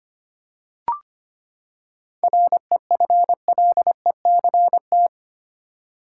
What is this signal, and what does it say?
Morse code, heard as a keyed tone, about 25 wpm, 700 Hz. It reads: REFLECT
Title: Seven Letter Words - 25wpm